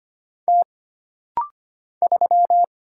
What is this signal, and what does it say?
Morse code, heard as a keyed tone, about 25 wpm, 700 Hz. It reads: T 3
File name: Single Character (Letter:Number:Punctuation:Prosign) - Rapid-Fire - 25wpm